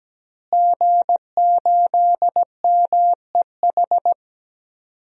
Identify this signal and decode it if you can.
Morse code, heard as a keyed tone, about 17 wpm, 700 Hz. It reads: G8MEH
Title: DXCC-100 Call Signs - 5 Characters - Repeated 3x - 17wpm